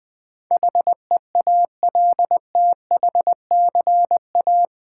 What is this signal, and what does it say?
Morse code, heard as a keyed tone, about 20 wpm, 700 Hz. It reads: HEALTHCA
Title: Build A Word - 4 Letters At A Time - 10-Letter Words - 20wpm